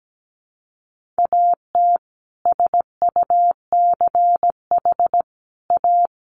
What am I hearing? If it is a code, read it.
Morse code, heard as a keyed tone, about 17 wpm, 700 Hz. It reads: AT SUCH A
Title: Sentences from Top 1000 Words 17wpm